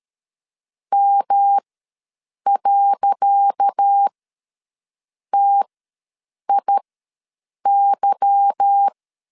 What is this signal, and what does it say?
Morse code, heard as a keyed tone, about 13 wpm, spaced as for 7 wpm, 800 Hz. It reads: M.TIY
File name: LEZIONI / 10 SIGNS / 10C DOT ws 12wpm 3548